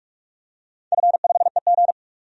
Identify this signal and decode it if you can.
Morse code, heard as a keyed tone, about 46 wpm, 700 Hz. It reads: F5EG
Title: DXCC-100 Call Signs - 4 Characters - 45wpm